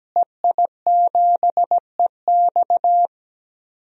Morse code, heard as a keyed tone, about 17 wpm, 700 Hz. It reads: EI7EX
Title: DXCC-100 Call Signs - 5 Characters - Repeated 3x - 17wpm